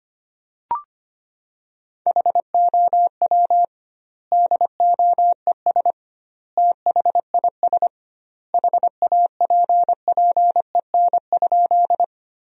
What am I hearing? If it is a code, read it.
Morse code, heard as a keyed tone, about 25 wpm, 700 Hz. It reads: HOW DOES THIS HAPPEN?
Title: 2x Word-Spacing - Sentences from Top 400 Words - 25wpm